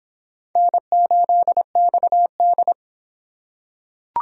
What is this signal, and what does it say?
Morse code, heard as a keyed tone, about 25 wpm, 700 Hz. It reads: N8XD